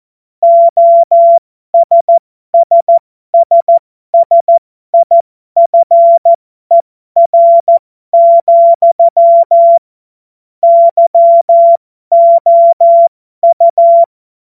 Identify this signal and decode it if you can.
Morse code, heard as a keyed tone, about 14 wpm, spaced as for 12 wpm, 680 Hz. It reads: OSSSSIFER, YOU
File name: Morsekurs / Witze